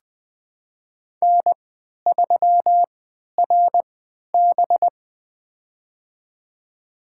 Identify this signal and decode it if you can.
Morse code, heard as a keyed tone, about 20 wpm, spaced as for 11 wpm, 700 Hz. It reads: N3RB
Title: USA Call Signs - 4 Characters - 3x Character Spacing - 20wpm